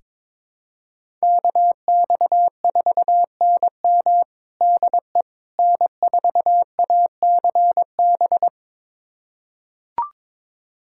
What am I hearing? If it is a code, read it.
Morse code, heard as a keyed tone, about 22 wpm, 700 Hz. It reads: KX4NM DE N4ACB
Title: USA Callsign DE Callsign 22wpm